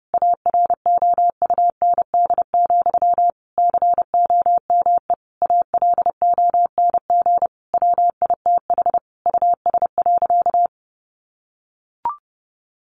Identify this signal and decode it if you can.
Morse code, heard as a keyed tone, about 28 wpm, 700 Hz. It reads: AROUND, COME ALONG WITH US.
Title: Sentences from Top 1200 Words 30wpm